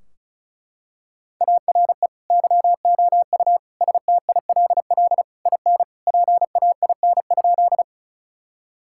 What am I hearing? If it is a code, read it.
Morse code, heard as a keyed tone, about 36 wpm, 700 Hz. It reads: ARE YOU STILL IN PAIN?
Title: Sentences from Top 1600 Words 35wpm